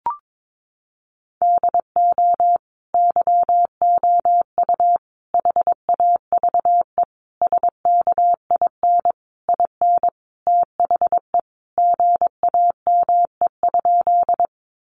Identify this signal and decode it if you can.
Morse code, heard as a keyed tone, about 22 wpm, 700 Hz. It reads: DO YOU HAVE SKIN IN THE GAME?
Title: Sentences from Top 1300 Words 22wpm